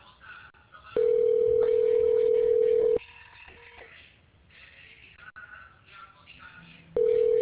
Ringback tone (North America), 440 + 480 Hz.